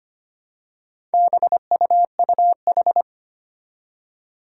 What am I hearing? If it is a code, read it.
Morse code, heard as a keyed tone, about 25 wpm, 700 Hz. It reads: BUUH
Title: Four Challenging Letter-Number Combinations 25wpm